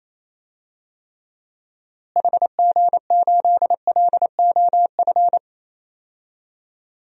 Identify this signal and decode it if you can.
Morse code, heard as a keyed tone, about 28 wpm, 700 Hz. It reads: HG8LOF
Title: DXCC-100 Call Signs - 6 Characters - Repeated 2x - 28wpm